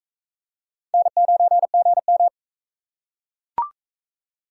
Morse code, heard as a keyed tone, about 44 wpm, 700 Hz. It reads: N9GM